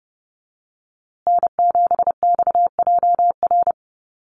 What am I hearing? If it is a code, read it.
Morse code, heard as a keyed tone, about 30 wpm, 700 Hz. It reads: N7XJR
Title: USA Call Signs Repeated 3x 30wpm